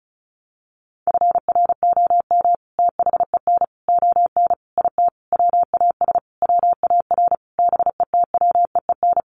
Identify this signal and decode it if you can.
Morse code, heard as a keyed tone, about 34 wpm, 700 Hz. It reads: FROM THEN ON IT WAS WAR BETWEEN
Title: Sentences from Top 400 Words 35wpm